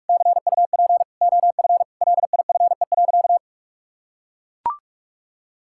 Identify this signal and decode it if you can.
Morse code, heard as a keyed tone, about 42 wpm, 700 Hz. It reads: CUP OF LIFE.
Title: Sentences from Top 1600 Words 45wpm